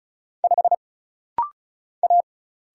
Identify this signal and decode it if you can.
Morse code, heard as a keyed tone, about 35 wpm, 700 Hz. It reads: 5 A
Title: Single Letter-Number - Rapid-Fire 35wpm